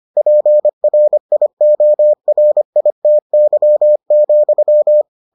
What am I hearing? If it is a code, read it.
Morse code, heard as a keyed tone, about 25 wpm, 600 Hz. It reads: PRIORITY,